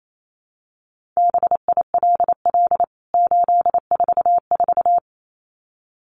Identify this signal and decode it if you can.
Morse code, heard as a keyed tone, about 28 wpm, 700 Hz. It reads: BILL 844
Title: CWT Exchange 28wpm